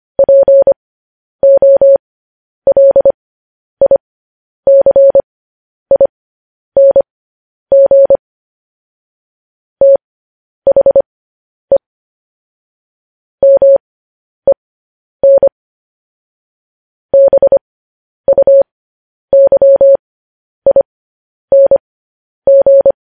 Morse code, heard as a keyed tone, about 25 wpm, spaced as for 10 wpm, 550 Hz. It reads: POLICING THE MEN BUYING
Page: News Headlines in Morse Code at 10 WPM